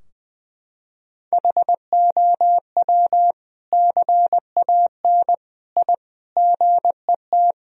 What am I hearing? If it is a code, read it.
Morse code, heard as a keyed tone, about 20 wpm, 700 Hz. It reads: HOW CAN I GET
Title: Sentences from Top 1000 Words - Review Part 3 - 20wpm